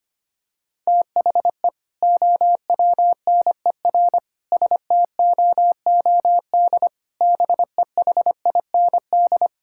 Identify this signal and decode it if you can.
Morse code, heard as a keyed tone, about 25 wpm, 700 Hz. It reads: THE OWNER STOOD BEHIND
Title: Sentences from Top 1600 Words 25wpm